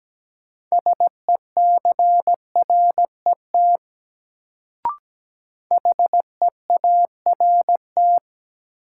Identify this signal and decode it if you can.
Morse code, heard as a keyed tone, about 17 wpm, 700 Hz. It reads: SECRET HEART
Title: Sentences from Top 1200 Words - New Word Drill Alt - 17wpm